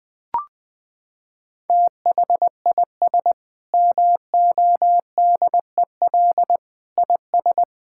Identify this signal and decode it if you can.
Morse code, heard as a keyed tone, about 21 wpm, 700 Hz. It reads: THIS MODEL IS
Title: Sentences with 8-Word Phrases - 20wpm